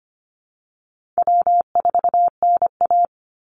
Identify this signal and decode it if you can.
Morse code, heard as a keyed tone, about 25 wpm, 700 Hz. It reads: W4NA
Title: Easy Call Signs Repeated 3x - 25wpm